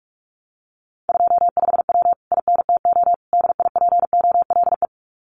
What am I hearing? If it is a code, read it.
Morse code, heard as a keyed tone, about 46 wpm, 700 Hz. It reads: 25W INTO DIPOLE
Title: Common QSO Phrases Encore 45wpm